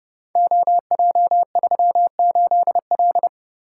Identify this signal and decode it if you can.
Morse code, heard as a keyed tone, about 30 wpm, 700 Hz. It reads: OJ38L
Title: Code Groups - 5 Letter:Numbers - 30wpm